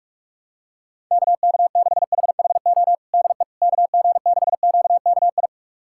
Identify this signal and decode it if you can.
Morse code, heard as a keyed tone, about 46 wpm, 700 Hz. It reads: KK6HHO DE KG6QKI
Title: USA Callsign DE Callsign 45wpm